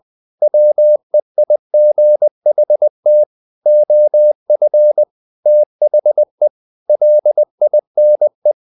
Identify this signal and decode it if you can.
Morse code, heard as a keyed tone, about 20 wpm, 600 Hz. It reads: WEIGHT OF THE LINE
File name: the old man and the sea part 2 20wpm